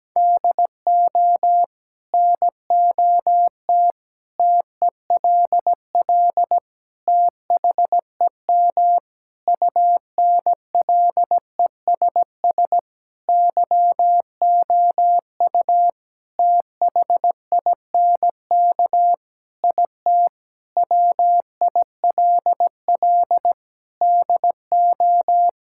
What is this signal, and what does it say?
Morse code, heard as a keyed tone, about 17 wpm, 700 Hz. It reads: DO NOT TELL THEM UNLESS YOU THINK IT WILL DO
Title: Sentences from Top 1000 Words - Review Part 6 - 17wpm